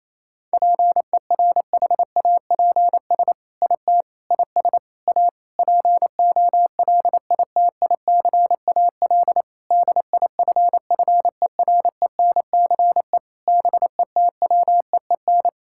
Morse code, heard as a keyed tone, about 28 wpm, 700 Hz. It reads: PERHAPS IT IS A POLITICAL DIFFERENCE BETWEEN
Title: Sentences from Top 1000 Words - Review Part 5 - 28wpm